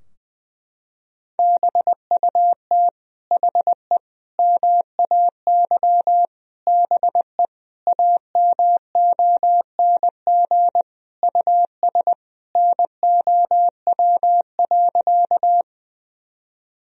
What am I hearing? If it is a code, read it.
Morse code, heard as a keyed tone, about 21 wpm, 700 Hz. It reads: BUT HE MAY BE AMONG US NOW.
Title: Sentences from Top 1000 Words - Review Part 3 - 20wpm